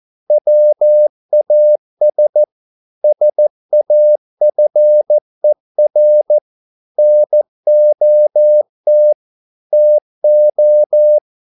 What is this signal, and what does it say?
Morse code, heard as a keyed tone, about 14 wpm, 600 Hz. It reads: WAS SAFER NOT TO